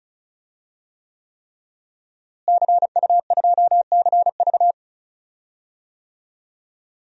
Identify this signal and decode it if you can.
Morse code, heard as a keyed tone, about 36 wpm, 700 Hz. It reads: CU2CV
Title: DXCC-100 Call Signs - 5 Characters - Repeated 3x - 35wpm